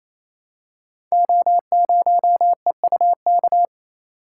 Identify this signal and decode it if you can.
Morse code, heard as a keyed tone, about 29 wpm, 700 Hz. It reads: O0EUK
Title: Code Groups - 5 Letter:Numbers - 28wpm